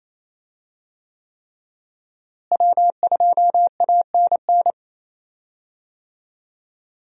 Morse code, heard as a keyed tone, about 28 wpm, 700 Hz. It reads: W2ANN
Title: USA Call Signs - 5 Characters - Repeated 3x - 28wpm